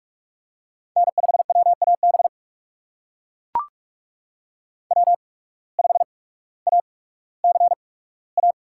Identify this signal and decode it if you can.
Morse code, heard as a keyed tone, about 44 wpm, spaced as for 19 wpm, 700 Hz. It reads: N5WAB W5ACA
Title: USA Call Signs - 5 Characters - 8x Character Spacing - 45wpm